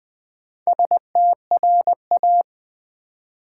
Morse code, heard as a keyed tone, about 20 wpm, 700 Hz. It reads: STRA
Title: Build A Word - 4 Letters At A Time - 8-Letter Words - 20wpm